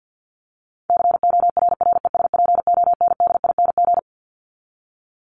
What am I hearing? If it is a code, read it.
Morse code, heard as a keyed tone, about 50 wpm, 700 Hz. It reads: CORRESPONDING
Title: ING Suffix - Top 100 Words with Base Words 50wpm